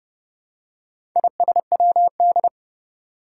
Morse code, heard as a keyed tone, about 30 wpm, 700 Hz. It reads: ISWD
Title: Four Challenging Letter-Number Combinations 30wpm